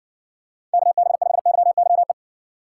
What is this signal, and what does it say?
Morse code, heard as a keyed tone, about 60 wpm, 700 Hz. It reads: KB5YYE